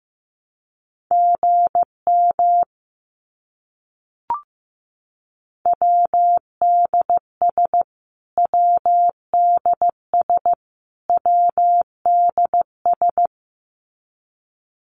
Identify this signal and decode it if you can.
Morse code, heard as a keyed tone, about 15 wpm, 700 Hz. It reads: GM WDS WDS WDS
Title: Abbreviations Repeated 3x - 15wpm